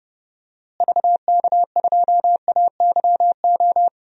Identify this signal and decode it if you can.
Morse code, heard as a keyed tone, about 30 wpm, 700 Hz. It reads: VK2AYO
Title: DXCC-100 Call Signs - 6 Characters - Repeated 3x - 30wpm